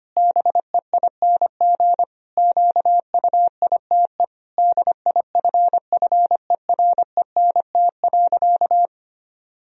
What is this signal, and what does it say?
Morse code, heard as a keyed tone, about 25 wpm, 700 Hz. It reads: BEING QUITE DIFFERENT.
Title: Sentences from Top 1000 Words - Review Part 3 - 25wpm